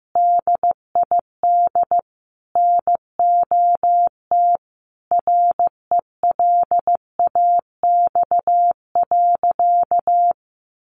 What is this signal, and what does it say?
Morse code, heard as a keyed tone, about 15 wpm, 700 Hz. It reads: DID NOT RELAX.
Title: Sentences from Top 1400 Words 15wpm